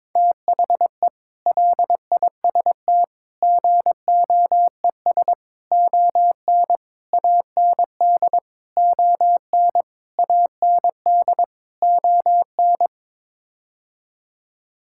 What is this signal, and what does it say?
Morse code, heard as a keyed tone, about 22 wpm, 700 Hz. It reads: THE LIST GOES ON AND ON AND ON
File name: practice / 22wpm / Sets of 8 Words - 22wpm